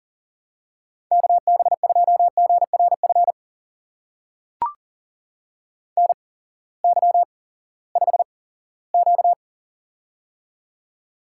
Morse code, heard as a keyed tone, about 40 wpm, spaced as for 20 wpm, 700 Hz. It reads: KB2GRF NY5Q
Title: USA Call Signs - 8x Character Spacing - 40wpm